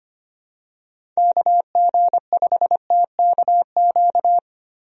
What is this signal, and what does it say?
Morse code, heard as a keyed tone, about 25 wpm, 700 Hz. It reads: KG5TKQ
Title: USA Call Signs - 6 Characters - Repeated 2x - 25wpm